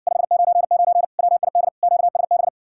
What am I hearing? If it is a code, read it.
Morse code, heard as a keyed tone, about 60 wpm, 700 Hz. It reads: 599 WID QSB